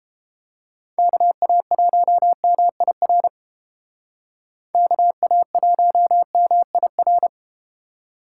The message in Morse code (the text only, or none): KA1MIR KA1MIR